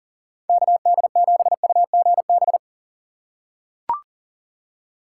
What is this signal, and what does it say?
Morse code, heard as a keyed tone, about 40 wpm, 690 Hz. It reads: KD7UGB
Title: USA Call Signs - 4x Character Spacing - 40wpm